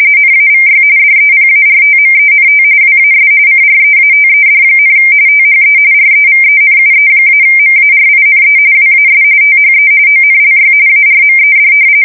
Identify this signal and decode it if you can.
RTTY, about 45.45 baud. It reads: NAVY DEPARTMENT WAS CREATED, THE REVENUE-MARINE WAS THE ONLY ARMED
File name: fsk170 / usnavy1